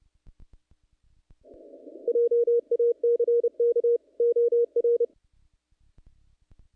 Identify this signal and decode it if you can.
Morse code, heard as a keyed tone, about 30 wpm, 470 Hz. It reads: JACK OR
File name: contests / naqp-cw-k5tr-op-k5pi- -2100